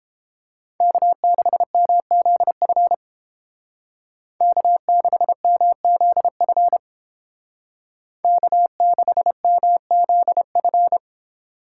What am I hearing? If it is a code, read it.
Morse code, heard as a keyed tone, about 30 wpm, 700 Hz. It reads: K6MZF K6MZF K6MZF